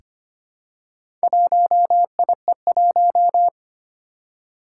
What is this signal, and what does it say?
Morse code, heard as a keyed tone, about 25 wpm, 700 Hz. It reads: 1IE1
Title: Four Challenging Letter-Number Combinations 25wpm